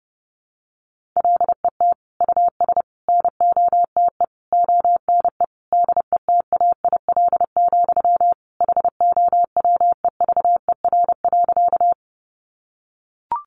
Morse code, heard as a keyed tone, about 30 wpm, 700 Hz. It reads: LET US NOTE ONE DETAIL, HOWEVER.
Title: Sentences from Top 1500 Words 30wpm